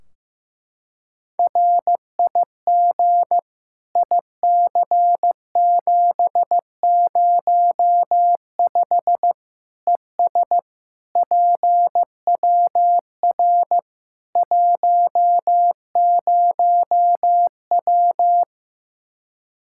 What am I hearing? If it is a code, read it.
Morse code, heard as a keyed tone, about 15 wpm, 700 Hz. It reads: RIG IC705 ES PWR 10W
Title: Practice QSO - Exchange 2 - Info - 15wpm